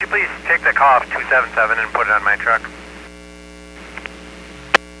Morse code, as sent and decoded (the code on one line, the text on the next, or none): none